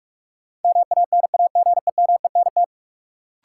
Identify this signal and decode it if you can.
Morse code, heard as a keyed tone, about 43 wpm, 700 Hz. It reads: MANAGEMENT
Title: Sentences from Top 1200 Words - New Word Drill Alt - 45wpm